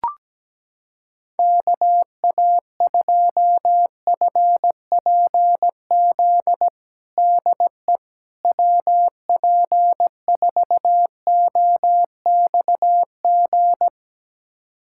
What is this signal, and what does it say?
Morse code, heard as a keyed tone, about 17 wpm, 700 Hz. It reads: KA2FPZ DE WP4OXG
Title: USA Callsign DE Callsign - 6 Characters - 17wpm